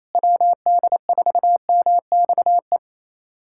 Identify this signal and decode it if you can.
Morse code, heard as a keyed tone, about 28 wpm, 700 Hz. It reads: WD4MXE
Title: USA Callsign DE Callsign 28wpm